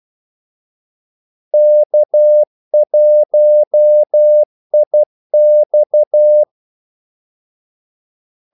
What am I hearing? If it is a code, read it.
Morse code, heard as a keyed tone, about 12 wpm, 600 Hz. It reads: K1IX
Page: 2 ekstra ordmellemrum mellem kaldesignalerne